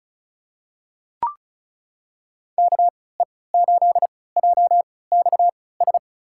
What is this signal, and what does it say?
Morse code, heard as a keyed tone, about 36 wpm, spaced as for 20 wpm, 700 Hz. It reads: KE8JXS
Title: USA Call Signs - 6 Characters - 3x Character Spacing - 35wpm